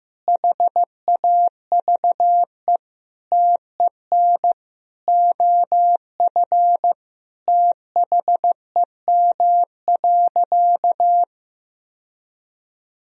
Morse code, heard as a keyed tone, about 15 wpm, 700 Hz. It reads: HAVE TEN OF THEM.
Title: Sentences from Top 1000 Words - Review Part 2 - 15wpm